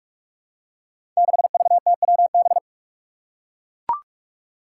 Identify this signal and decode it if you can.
Morse code, heard as a keyed tone, about 43 wpm, 700 Hz. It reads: 6VTWB
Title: Five Challenging Letter-Number Combinations 45wpm